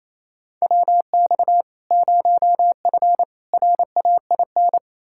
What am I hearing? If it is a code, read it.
Morse code, heard as a keyed tone, about 28 wpm, 700 Hz. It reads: WX 0F RAIN
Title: Practice QSO - Exchange 2 - Info - 28wpm